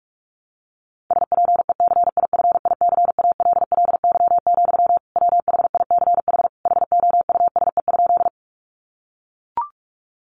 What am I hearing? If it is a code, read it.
Morse code, heard as a keyed tone, about 44 wpm, 700 Hz. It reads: SPECIFICALLY, WHICH HOUSE?
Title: Sentences from Top 1100 Words 45wpm